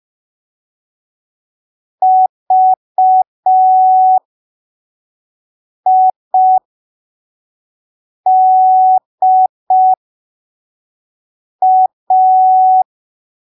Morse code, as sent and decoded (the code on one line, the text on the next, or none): ...- .. -.. .-
VIDA